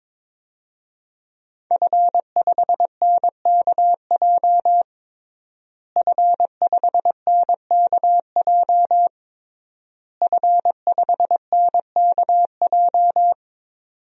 Morse code, heard as a keyed tone, about 23 wpm, 700 Hz. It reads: F5NKJ F5NKJ F5NKJ
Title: DXCC-100 Call Signs - 5 Characters - Repeated 3x - 22wpm